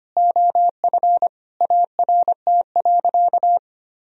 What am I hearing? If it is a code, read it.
Morse code, heard as a keyed tone, about 25 wpm, 700 Hz. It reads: OF ART.
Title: Sentences from Top 1000 Words - Review Part 4 - 25wpm